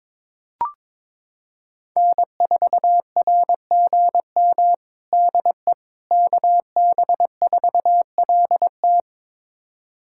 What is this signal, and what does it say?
Morse code, heard as a keyed tone, about 22 wpm, 700 Hz. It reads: N4RGM DE KB4LT
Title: USA Callsign DE Callsign - 5 Characters - 22wpm